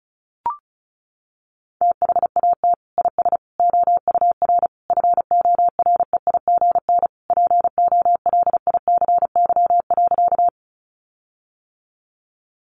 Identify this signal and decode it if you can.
Morse code, heard as a keyed tone, about 34 wpm, 700 Hz. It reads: THAT IS OUR FOREIGN POLICY.
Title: Sentences from Top 1400 Words 35wpm